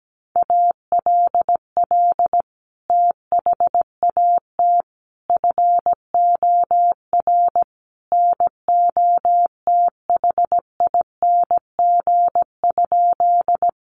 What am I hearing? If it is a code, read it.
Morse code, heard as a keyed tone, about 17 wpm, 700 Hz. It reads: ALL THAT FOR NOTHING?
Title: Sentences from Top 500 Words 17wpm